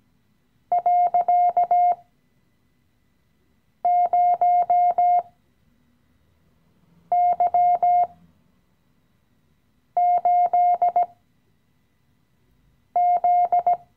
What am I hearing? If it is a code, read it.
Morse code, heard as a keyed tone, about 17 wpm, spaced as for 4 wpm, 710 Hz. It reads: .0Y8Z
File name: CW-Online-Kurs / Anfänger / Lektion 015